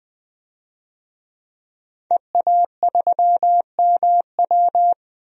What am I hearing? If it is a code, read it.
Morse code, heard as a keyed tone, about 20 wpm, 700 Hz. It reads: EA3MW
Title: DXCC-100 Call Signs - 5 Characters - 20wpm